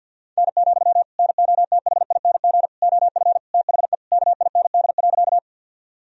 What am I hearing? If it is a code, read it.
Morse code, heard as a keyed tone, about 53 wpm, 700 Hz. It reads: N, NOTHING OF THE KIND.
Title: Sentences from Top 500 Words and Abbreviations 50wpm